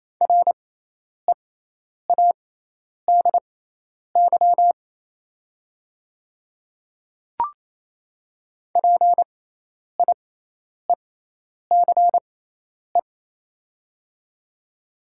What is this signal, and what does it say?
Morse code, heard as a keyed tone, about 28 wpm, spaced as for 10 wpm, 700 Hz. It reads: READY PIECE
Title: 6x Character Spacing - Top 500 Words - 28wpm